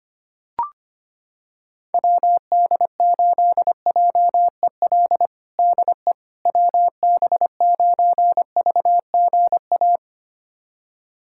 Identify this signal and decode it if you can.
Morse code, heard as a keyed tone, about 25 wpm, 700 Hz. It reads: WD8JEL DE WB9VGA
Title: USA Callsign DE Callsign - 6 Characters - 25wpm